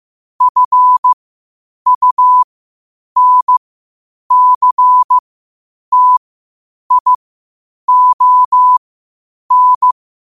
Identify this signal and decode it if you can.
Morse code, heard as a keyed tone, about 15 wpm, spaced as for 8 wpm, 1000 Hz. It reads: FUNCTION
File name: qotdfiles / 8wpm